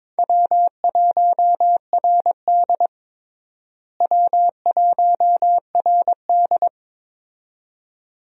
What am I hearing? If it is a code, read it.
Morse code, heard as a keyed tone, about 21 wpm, 700 Hz. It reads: W1RD W1RD
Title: USA Call Signs - 4 Characters - Repeated 2x - 22wpm